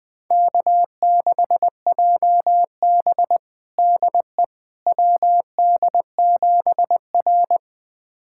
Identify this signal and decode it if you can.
Morse code, heard as a keyed tone, about 20 wpm, 700 Hz. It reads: K6JB DE WD7R
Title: USA Callsign DE Callsign - 4 Characters - 20wpm